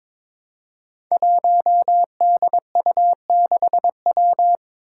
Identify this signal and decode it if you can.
Morse code, heard as a keyed tone, about 22 wpm, 700 Hz. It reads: 1DU6W
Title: Five Challenging Letter-Number Combinations 22wpm